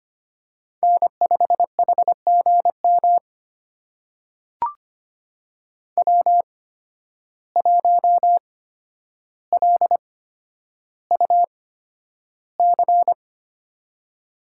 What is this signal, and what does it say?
Morse code, heard as a keyed tone, about 25 wpm, spaced as for 11 wpm, 700 Hz. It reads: N5HGM W1LUC
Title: USA Call Signs - 5 Characters - 8x Character Spacing - 25wpm